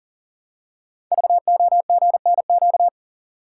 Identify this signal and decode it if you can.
Morse code, heard as a keyed tone, about 38 wpm, 700 Hz. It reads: VOGNQ
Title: Code Groups - 5 Letters - 40wpm